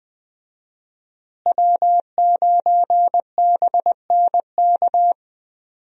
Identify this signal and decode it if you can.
Morse code, heard as a keyed tone, about 20 wpm, 700 Hz. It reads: W9BNK
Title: USA Call Signs - 5 Characters - Repeated 2x - 20wpm